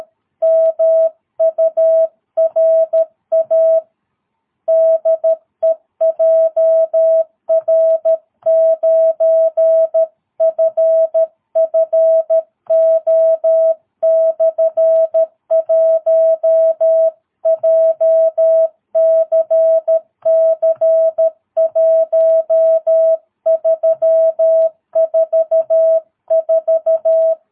Morse code, heard as a keyed tone, about 13 wpm, 650 Hz. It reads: MURA DE JR9FFO/1 JCC1344
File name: jh8cbh / 2022cw